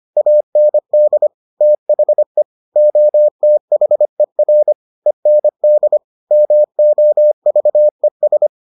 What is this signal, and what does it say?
Morse code, heard as a keyed tone, about 25 wpm, 600 Hz. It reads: AND THE OTHER END MOVES